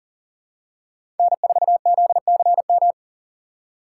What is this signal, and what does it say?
Morse code, heard as a keyed tone, about 40 wpm, 700 Hz. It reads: N4ZCM
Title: USA Call Signs - 5 Characters - 4x Character Spacing - 40wpm